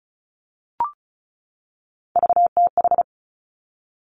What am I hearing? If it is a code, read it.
Morse code, heard as a keyed tone, about 36 wpm, 700 Hz. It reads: VTH
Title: Three Challenging Letter-Number Combinations 35wpm